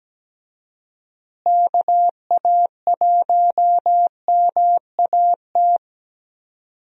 Morse code, heard as a keyed tone, about 17 wpm, 700 Hz. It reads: KA1MAT
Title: USA Call Signs - 6 Characters - Repeated 3x - 17wpm